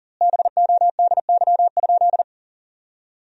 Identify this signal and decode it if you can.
Morse code, heard as a keyed tone, about 40 wpm, 700 Hz. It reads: BODY?
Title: 2x Word-Spacing - Sentences from Top 400 Words - 40wpm